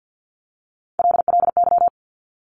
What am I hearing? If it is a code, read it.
Morse code, heard as a keyed tone, about 50 wpm, 700 Hz. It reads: LLY
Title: Top Three Letter Combinations in Words 50wpm